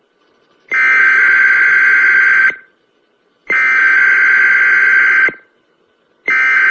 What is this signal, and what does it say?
Emergency Alert System (SAME) tones (header bursts), heard as harsh buzzing bursts. It is heard from 0.7 to 5.6 s.